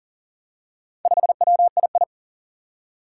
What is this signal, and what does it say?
Morse code, heard as a keyed tone, about 38 wpm, 700 Hz. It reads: 5WII
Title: Four Challenging Letter-Number Combinations 40wpm